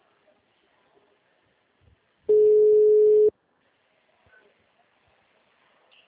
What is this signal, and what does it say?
Ringback tone (Europe, 425 Hz).
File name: out- -2018- - - .26961